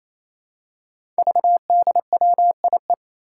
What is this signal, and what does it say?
Morse code, heard as a keyed tone, about 27 wpm, 700 Hz. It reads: VDWIE
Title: Five Challenging Letter-Number Combinations 28wpm